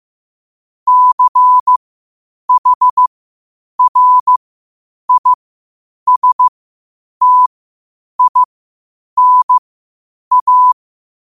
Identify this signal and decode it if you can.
Morse code, heard as a keyed tone, about 15 wpm, spaced as for 8 wpm, 1000 Hz. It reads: CHRISTINA
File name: qotdfiles / 8wpm